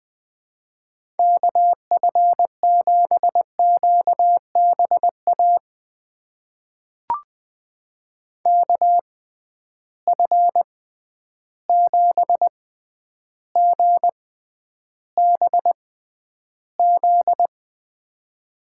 Morse code, heard as a keyed tone, about 21 wpm, spaced as for 10 wpm, 700 Hz. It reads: KF7QBA KF7GBZ
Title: USA Call Signs - 6 Characters - 6x Character Spacing - 20wpm